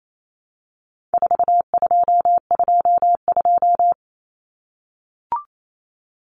Morse code, heard as a keyed tone, about 28 wpm, 700 Hz. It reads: 4222
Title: Common Cut Numbers - 4-Digit 28wpm